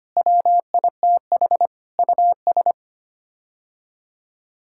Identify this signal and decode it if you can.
Morse code, heard as a keyed tone, about 25 wpm, 700 Hz. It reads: WITH US
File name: Sets of 9 Words - 25wpm